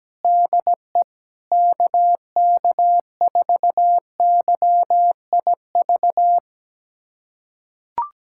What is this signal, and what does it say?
Morse code, heard as a keyed tone, about 17 wpm, 700 Hz. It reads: DE KK4YIV
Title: USA Callsign DE Callsign - 6 Characters - 17wpm